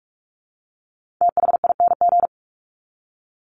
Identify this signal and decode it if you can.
Morse code, heard as a keyed tone, about 44 wpm, 700 Hz. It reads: THING
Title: Top 300 Words 45wpm